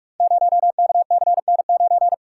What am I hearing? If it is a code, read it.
Morse code, heard as a keyed tone, about 44 wpm, 690 Hz. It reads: 0KCN9
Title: Code Groups - 5 Letter:Numbers - 45wpm